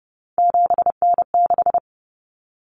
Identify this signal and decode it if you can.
Morse code, heard as a keyed tone, about 30 wpm, 700 Hz. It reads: 7N6
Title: Common Cut Numbers - 3-Digit 30wpm